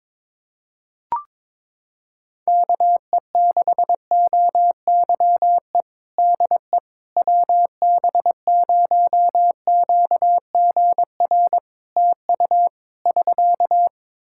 Morse code, heard as a keyed tone, about 22 wpm, 700 Hz. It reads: KE6OYE DE WB0QGR TU <SK>
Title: Practice QSO - Exchange 4 - Ending - 22wpm